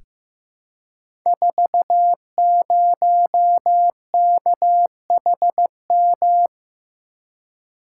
Morse code, heard as a keyed tone, about 15 wpm, 700 Hz. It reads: 40KHM
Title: Code Groups - 5 Letter:Numbers - 15wpm